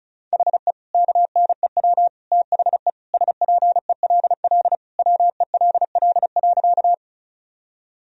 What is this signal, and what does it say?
Morse code, heard as a keyed tone, about 34 wpm, 700 Hz. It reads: HE KNEW THE SPELL WELL.
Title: Sentences from Top 1000 Words - Review Part 1 - 35wpm